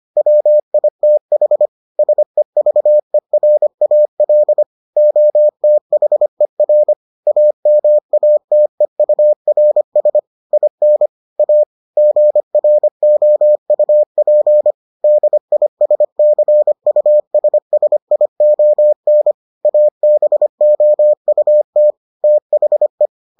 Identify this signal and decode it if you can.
Morse code, heard as a keyed tone, about 25 wpm, 600 Hz. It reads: WITH SEVERAL OTHER AMATEURS IN A GROUP DISCUSSION ABOUT THE